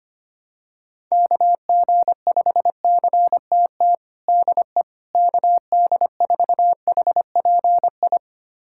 Morse code, heard as a keyed tone, about 25 wpm, 700 Hz. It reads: KG5CTT DE KD4HPI
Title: USA Callsign DE Callsign 25wpm